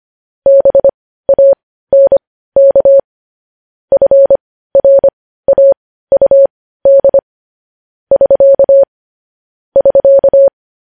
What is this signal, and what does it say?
Morse code, heard as a keyed tone, about 25 wpm, spaced as for 15 wpm, 550 Hz. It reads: BANK FRAUD <SK> <SK>